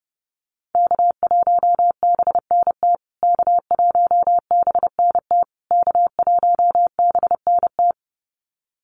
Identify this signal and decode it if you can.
Morse code, heard as a keyed tone, about 30 wpm, 700 Hz. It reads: K1BNT K1BNT K1BNT
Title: USA Call Signs Repeated 3x 30wpm